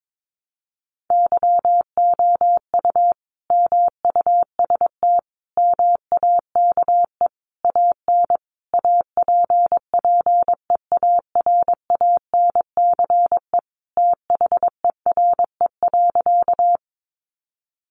Morse code, heard as a keyed tone, about 22 wpm, 700 Hz. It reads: YOU MUST MAKE AN APPEARANCE THERE.
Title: Sentences from Top 1100 Words 22wpm